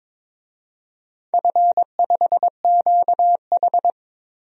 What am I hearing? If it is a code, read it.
Morse code, heard as a keyed tone, about 22 wpm, 700 Hz. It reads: F5QH
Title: DXCC-100 Call Signs - 4 Characters - Repeated 3x - 22wpm